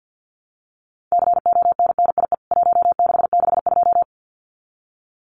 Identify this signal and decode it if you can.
Morse code, heard as a keyed tone, about 50 wpm, 700 Hz. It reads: CONNIE 1662
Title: CWT Exchange 50wpm